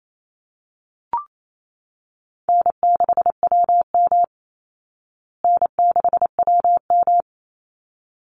Morse code, heard as a keyed tone, about 28 wpm, 700 Hz. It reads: N6WM N6WM
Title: Easy Call Signs Repeated 2x - 28wpm